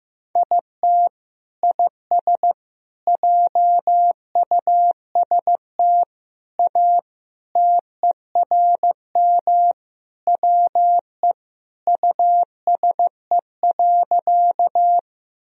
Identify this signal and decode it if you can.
Morse code, heard as a keyed tone, about 14 wpm, 700 Hz. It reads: IT IS JUST A TERM WE USE.
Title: Sentences from Top 1000 Words - Review Part 4 - 15wpm